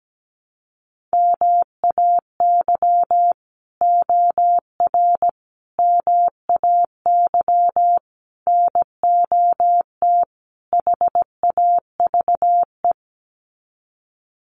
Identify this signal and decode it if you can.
Morse code, heard as a keyed tone, about 17 wpm, 700 Hz. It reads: MAY OR MAY NOT HAVE
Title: Sets of 5 Words Encore2 17wpm